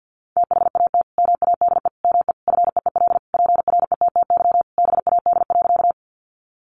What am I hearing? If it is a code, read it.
Morse code, heard as a keyed tone, about 50 wpm, 700 Hz. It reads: THAT MADE ME FEEL PRETTY BAD.
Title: Sentences from Top 1000 Words 50wpm